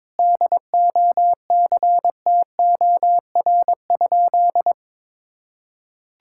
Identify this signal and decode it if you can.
Morse code, heard as a keyed tone, about 22 wpm, 700 Hz. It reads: DOCTOR?
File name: practice / 22wpm / Sentences with 4-Word Phrases - 22wpm